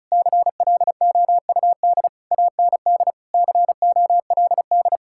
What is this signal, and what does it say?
Morse code, heard as a keyed tone, about 44 wpm, 700 Hz. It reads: CLOUD AND COLD